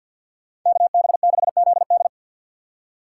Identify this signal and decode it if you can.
Morse code, heard as a keyed tone, about 52 wpm, 700 Hz. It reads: KB6ZD
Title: USA Call Signs - 4x Character Spacing - 50wpm